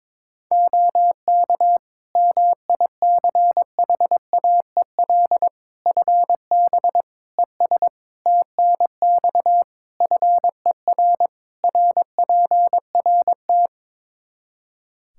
Morse code, heard as a keyed tone, about 22 wpm, 700 Hz. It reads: OK MICHAEL FB ES TNX FER RPRT
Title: K1USN SST QSO Practice 22wpm